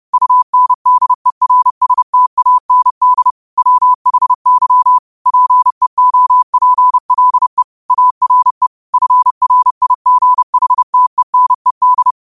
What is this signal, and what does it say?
Morse code, heard as a keyed tone, about 30 wpm, 1000 Hz. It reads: UNDERSTAND WHY PEOPLE ARE FRIGHTENED